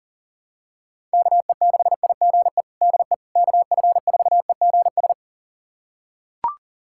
Morse code, heard as a keyed tone, about 37 wpm, 700 Hz. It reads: KE6IGE DE KF4EGS
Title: USA Callsign DE Callsign - 6 Characters - 40wpm